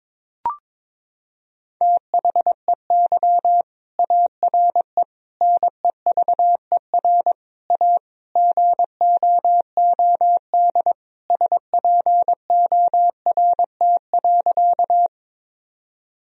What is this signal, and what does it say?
Morse code, heard as a keyed tone, about 21 wpm, 700 Hz. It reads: THEY ARE NEVER A GOOD SPORT.
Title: Sentences from Top 1000 Words - Review Part 4 - 22wpm